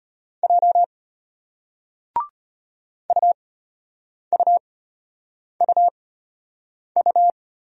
Morse code, heard as a keyed tone, about 33 wpm, spaced as for 8 wpm, 700 Hz. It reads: J UUUU